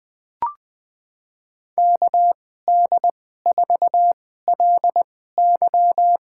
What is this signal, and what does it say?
Morse code, heard as a keyed tone, about 20 wpm, spaced as for 14 wpm, 700 Hz. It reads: KD4LY
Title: USA Call Signs - 2x Character Spacing - 20wpm